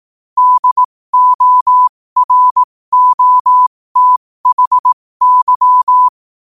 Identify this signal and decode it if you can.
Morse code, heard as a keyed tone, about 18 wpm, spaced as for 15 wpm, 1000 Hz. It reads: DOROTHY
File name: qotdfiles / 15wpm